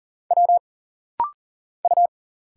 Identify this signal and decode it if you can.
Morse code, heard as a keyed tone, about 40 wpm, 700 Hz. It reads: W U
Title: Single Letters - Rapid-Fire 40wpm